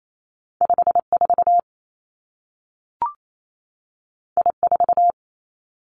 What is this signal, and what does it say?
Morse code, heard as a keyed tone, about 28 wpm, 700 Hz. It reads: 5 4 I 4
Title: Two Challenging Letter-Number Combinations 28wpm